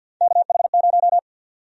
Morse code, heard as a keyed tone, about 51 wpm, 700 Hz. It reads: KH0
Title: USA Call Sign Prefixes - 50wpm